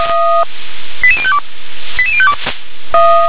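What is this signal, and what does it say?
Five-tone selective call (ZVEI), heard as quick runs of five tones, tones of about 70 ms. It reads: 88052 88052
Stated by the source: ZVEI five-tone call 88101 received by SDR.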